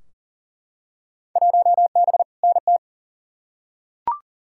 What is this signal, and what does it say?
Morse code, heard as a keyed tone, about 40 wpm, 700 Hz. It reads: 1B NT
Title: Field Day Exchanges Encore - 40wpm